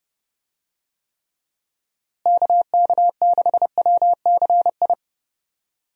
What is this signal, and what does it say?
Morse code, heard as a keyed tone, about 32 wpm, 700 Hz. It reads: KK6WCI
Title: USA Call Signs - 6 Characters - Repeated 2x - 30wpm